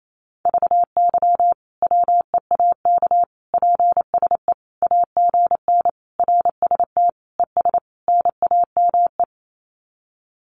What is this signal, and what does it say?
Morse code, heard as a keyed tone, about 28 wpm, 700 Hz. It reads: VY WEAK PSE AGN RST ES NAME
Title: Common QSO Phrases 28wpm